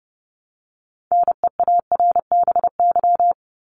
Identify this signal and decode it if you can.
Morse code, heard as a keyed tone, about 30 wpm, 700 Hz. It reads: NEARBY
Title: Top 1501-1750 Words 30wpm